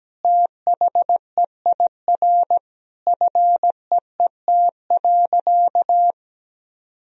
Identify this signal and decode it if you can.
Morse code, heard as a keyed tone, about 17 wpm, 700 Hz. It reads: THEIR FEET.
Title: Sentences from Top 1000 Words - Review Part 3 - 17wpm